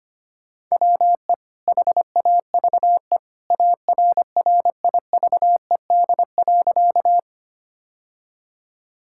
Morse code, heard as a keyed tone, about 25 wpm, 700 Hz. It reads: WE HAVE ARRIVED.
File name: practice / 25wpm / Sentences with 2-Word Phrases - 25wpm